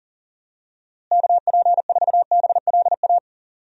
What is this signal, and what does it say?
Morse code, heard as a keyed tone, about 40 wpm, 700 Hz. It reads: KP4BLA
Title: DXCC-100 Call Signs - 6 Characters - Repeated 3x - 40wpm